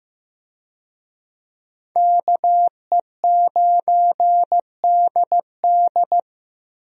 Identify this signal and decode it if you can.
Morse code, heard as a keyed tone, about 16 wpm, 700 Hz. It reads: KE9DD
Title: USA Call Signs - 5 Characters - Repeated 3x - 15wpm